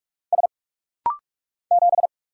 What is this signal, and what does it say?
Morse code, heard as a keyed tone, about 44 wpm, 700 Hz. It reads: S 7
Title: Single Letter-Number Alt - Rapid-Fire 45wpm